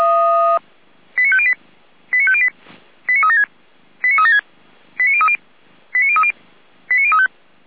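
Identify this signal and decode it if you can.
Five-tone selective call (ZVEI), heard as quick runs of five tones, tones of about 70 ms. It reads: 89508 89508 89376 89376 89030 89030 89035
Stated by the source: Selective call received by SDR, logged by Feuerwehr Orlamünde.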